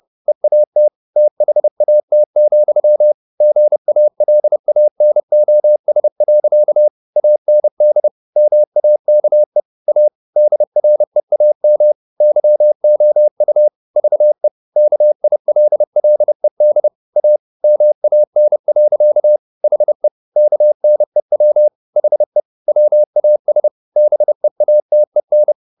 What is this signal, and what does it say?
Morse code, heard as a keyed tone, about 30 wpm, 600 Hz. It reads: EAT THAT, GALANOS. AND MAKE A DREAM YOU VE KILLED A MAN. HE KNEW HE WAS BEATEN